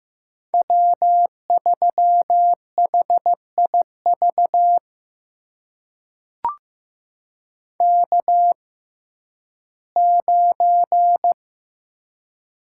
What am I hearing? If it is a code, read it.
Morse code, heard as a keyed tone, about 15 wpm, spaced as for 11 wpm, 700 Hz. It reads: W3HIV K9
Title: USA Call Signs - 5 Characters - 6x Character Spacing - 15wpm